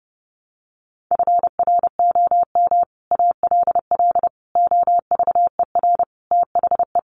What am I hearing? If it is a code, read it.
Morse code, heard as a keyed tone, about 30 wpm, 700 Hz. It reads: FROM ALL OVER THE
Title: Sets of 5 Words Encore 30wpm